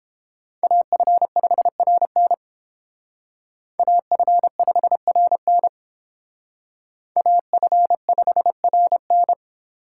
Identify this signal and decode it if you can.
Morse code, heard as a keyed tone, about 31 wpm, 700 Hz. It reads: AF5RN AF5RN AF5RN